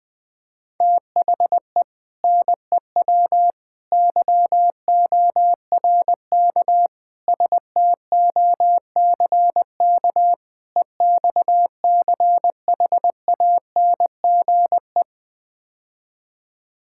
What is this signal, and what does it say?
Morse code, heard as a keyed tone, about 21 wpm, 700 Hz. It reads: THE NEW YORK STOCK EXCHANGE
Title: Build A Phrase - 5-Word Phrases - 20wpm